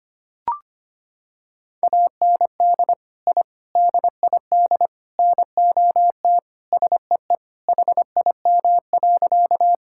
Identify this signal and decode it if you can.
Morse code, heard as a keyed tone, about 25 wpm, 700 Hz. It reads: AND I DID NOT SEE HIM.
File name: practice / 25wpm / Sentences with 4-Word Phrases - 25wpm